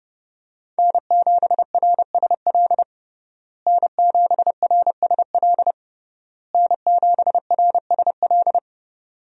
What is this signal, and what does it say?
Morse code, heard as a keyed tone, about 30 wpm, 700 Hz. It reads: N7RSL N7RSL N7RSL
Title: USA Call Signs - 5 Characters - Repeated 3x - 30wpm